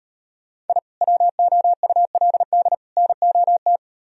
Morse code, heard as a keyed tone, about 38 wpm, 700 Hz. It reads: I WOULD NOT